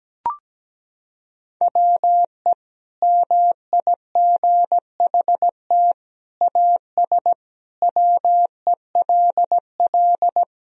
Morse code, heard as a keyed tone, about 17 wpm, 700 Hz. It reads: WE MIGHT AS WELL
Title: Sentences from Top 1000 Words - Review Part 1 - 17wpm